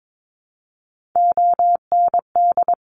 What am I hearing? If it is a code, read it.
Morse code, heard as a keyed tone, about 22 wpm, 700 Hz. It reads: OND
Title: Top Three Letter Combinations in Words 22wpm